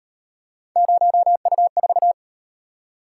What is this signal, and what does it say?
Morse code, heard as a keyed tone, about 37 wpm, 700 Hz. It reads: 0U4